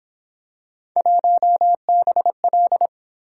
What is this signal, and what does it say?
Morse code, heard as a keyed tone, about 27 wpm, 700 Hz. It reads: 1BL
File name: Speed-Racing - Three Letter-Number Combinations - 22wpm